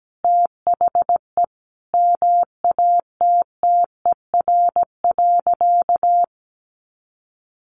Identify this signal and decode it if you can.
Morse code, heard as a keyed tone, about 17 wpm, 700 Hz. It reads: THE MATTER.
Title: Sentences from Top 1300 Words 17wpm